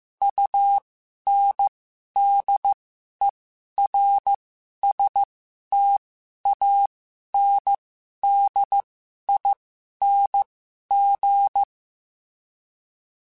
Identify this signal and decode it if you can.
Morse code, heard as a keyed tone, about 15 wpm, spaced as for 11 wpm, 800 Hz. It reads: UNDERSTANDING
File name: Special INGs-mp3s / Special INGs-mp3s / Sngl Ing @10